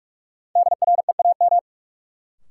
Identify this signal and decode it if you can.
Morse code, heard as a keyed tone, about 46 wpm, 700 Hz. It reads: DREAM
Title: Sentences from Top 1500 Words - New Word Drill Alt - 45wpm